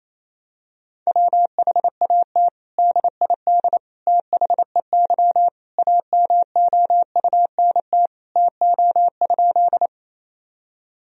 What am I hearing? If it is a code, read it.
Morse code, heard as a keyed tone, about 28 wpm, 700 Hz. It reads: WHAT DID THEY AMOUNT TO?
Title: Sentences from Top 1000 Words - Review Part 4 - 28wpm